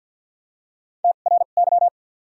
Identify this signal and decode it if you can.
Morse code, heard as a keyed tone, about 53 wpm, spaced as for 37 wpm, 700 Hz. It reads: TRY
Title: 2x Character Spacing - Top 300 Words - 50wpm